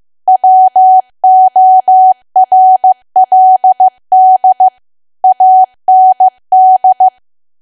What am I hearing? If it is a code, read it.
Morse code, heard as a keyed tone, about 15 wpm, 750 Hz. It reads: WORLD AND
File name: cwa / W1AW-Level2 / 15WPM